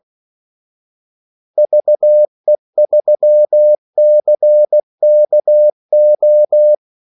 Morse code, heard as a keyed tone, about 16 wpm, 600 Hz. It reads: VE3CKO
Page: Kaldesignaler 16 wpm